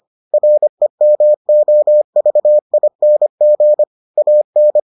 Morse code, heard as a keyed tone, about 25 wpm, 600 Hz. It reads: REMOVING AN